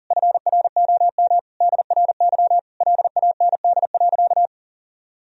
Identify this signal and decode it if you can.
Morse code, heard as a keyed tone, about 40 wpm, 700 Hz. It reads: FROM DRY LAND.
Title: Sentences from Top 1000 Words - Review Part 3 - 40wpm